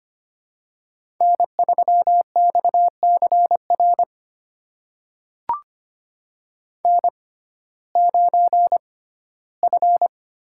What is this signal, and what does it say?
Morse code, heard as a keyed tone, about 25 wpm, spaced as for 15 wpm, 700 Hz. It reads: N3XCR N9F
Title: USA Call Signs - 5 Characters - 6x Character Spacing - 25wpm